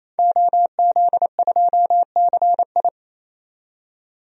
Morse code, heard as a keyed tone, about 28 wpm, 700 Hz. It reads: OZ2CI
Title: DXCC-100 Call Signs - 5 Characters - 28wpm